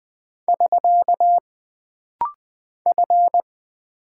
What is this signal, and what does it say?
Morse code, heard as a keyed tone, about 20 wpm, 700 Hz. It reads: <SK> F
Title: Single Character (Letter:Number:Punctuation:Prosign) - Rapid-Fire - 20wpm